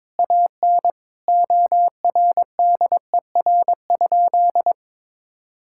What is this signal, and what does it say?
Morse code, heard as a keyed tone, about 22 wpm, 700 Hz. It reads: AN ORDER?
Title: Sentences from Top 1600 Words 22wpm